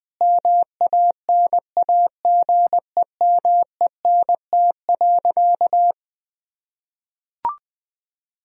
Morse code, heard as a keyed tone, about 21 wpm, 700 Hz. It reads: MANAGEMENT.
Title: Sentences from Top 1000 Words - Review Part 4 - 20wpm